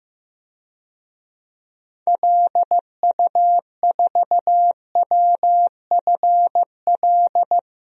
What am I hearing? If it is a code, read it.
Morse code, heard as a keyed tone, about 16 wpm, 700 Hz. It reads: LU4WFL
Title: DXCC-100 Call Signs - 6 Characters - Repeated 2x - 15wpm